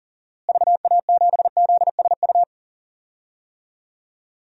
Morse code, heard as a keyed tone, about 40 wpm, 700 Hz. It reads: VA7ZSU
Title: DXCC-100 Call Signs - 6 Characters - Repeated 2x - 40wpm